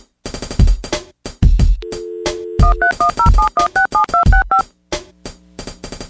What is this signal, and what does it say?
DTMF, heard as telephone touch tones, tones of about 110 ms. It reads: 13107167265